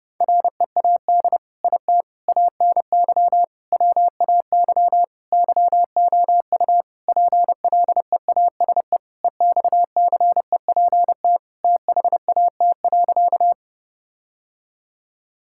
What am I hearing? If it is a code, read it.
Morse code, heard as a keyed tone, about 30 wpm, 700 Hz. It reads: READ IT ANY WAY YOU PLEASE EXCEPT THAT.
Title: Sentences from Top 1600 Words 30wpm